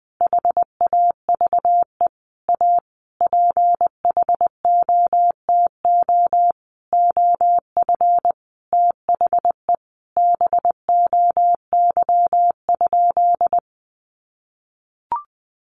Morse code, heard as a keyed tone, about 20 wpm, 700 Hz. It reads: HAVE A PHOTO OF THE BOY?
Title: Sentences from Top 1200 Words 20wpm